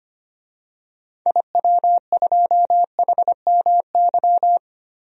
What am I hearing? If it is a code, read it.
Morse code, heard as a keyed tone, about 25 wpm, 700 Hz. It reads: IW2HMY
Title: DXCC-100 Call Signs - 6 Characters - Repeated 2x - 25wpm